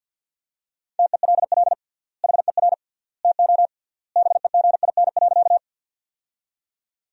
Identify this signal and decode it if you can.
Morse code, heard as a keyed tone, about 50 wpm, 700 Hz. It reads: TELL HER TO BEGIN.
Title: 3x Word-Spacing - Sentences from Top 300 Words - 50wpm